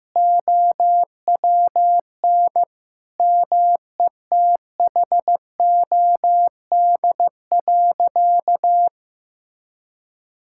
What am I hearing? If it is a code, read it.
Morse code, heard as a keyed tone, about 14 wpm, 700 Hz. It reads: OWN METHOD.
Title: Sentences from Top 1000 Words - Review Part 6 - 15wpm